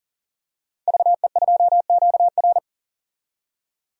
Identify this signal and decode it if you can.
Morse code, heard as a keyed tone, about 40 wpm, 700 Hz. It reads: VE2QR
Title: DXCC-100 Call Signs - 5 Characters - 40wpm